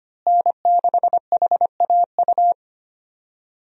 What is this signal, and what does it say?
Morse code, heard as a keyed tone, about 25 wpm, 700 Hz. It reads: N6HAU
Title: USA Call Signs - 5 Characters - Repeated 2x - 25wpm